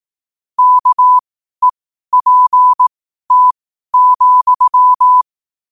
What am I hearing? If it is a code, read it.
Morse code, heard as a keyed tone, about 18 wpm, spaced as for 12 wpm, 1000 Hz. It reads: KEPT,